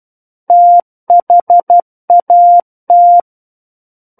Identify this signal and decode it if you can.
Morse code, heard as a keyed tone, about 12 wpm, 700 Hz. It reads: THAT
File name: Book One / 03-Chapter 3 - 12 WPM